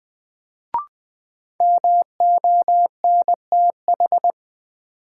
Morse code, heard as a keyed tone, about 20 wpm, 700 Hz. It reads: MONTH
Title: Sentences from Top 800 Words - New Word Drill - 20wpm